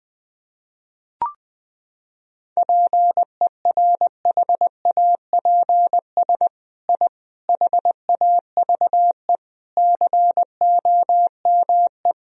Morse code, heard as a keyed tone, about 20 wpm, 700 Hz. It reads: PERHAPS I HAVE COME
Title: Sentences from Top 1000 Words - Review Part 2 - 20wpm